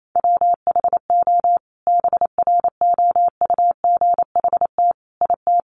Morse code, heard as a keyed tone, about 28 wpm, 700 Hz. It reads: WHO BROUGHT IT
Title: Sentences from Top 600 Words 28wpm